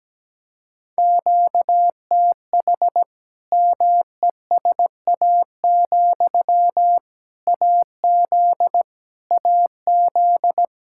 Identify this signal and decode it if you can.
Morse code, heard as a keyed tone, about 17 wpm, 700 Hz. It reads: QTH MESA, AZ AZ
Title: Practice QSO - Exchange 1 - Beginning - 17wpm